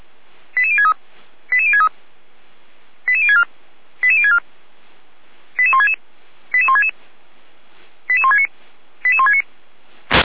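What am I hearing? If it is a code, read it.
Five-tone selective call (ZVEI), heard as quick runs of five tones, tones of about 70 ms. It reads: 88063 88063 88064 88064 88166 88166 88169 88169
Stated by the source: ZVEI five-tone call received by SDR.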